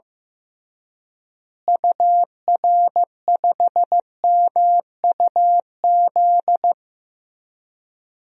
Morse code, heard as a keyed tone, about 15 wpm, 700 Hz. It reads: UR5MUZ
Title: DXCC-100 Call Signs - 6 Characters - Repeated 2x - 15wpm